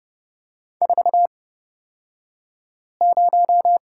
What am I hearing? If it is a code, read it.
Morse code, heard as a keyed tone, about 30 wpm, spaced as for 5 wpm, 700 Hz. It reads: 40
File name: Lesson 38d - Letter q - Two Characters 30wpm